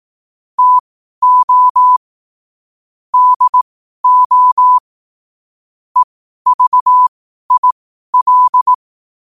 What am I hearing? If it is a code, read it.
Morse code, heard as a keyed tone, about 18 wpm, spaced as for 12 wpm, 1000 Hz. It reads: TO DO EVIL
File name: qotdfiles / 12wpm